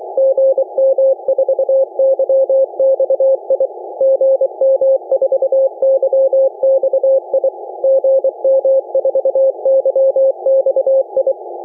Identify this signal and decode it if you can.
Morse code, heard as a keyed tone, about 23 wpm, 540 Hz. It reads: GM4YXI GM4YXI GM4YXI